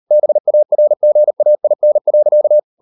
Morse code, heard as a keyed tone, about 38 wpm, 600 Hz. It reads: BARGAIN.